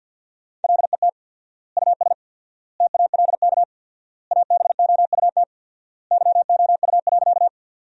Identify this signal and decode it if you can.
Morse code, heard as a keyed tone, about 50 wpm, 700 Hz. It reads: LET US TALK ABOUT YOU.
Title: 4x Word-Spacing - Sentences from Top 400 Words - 50wpm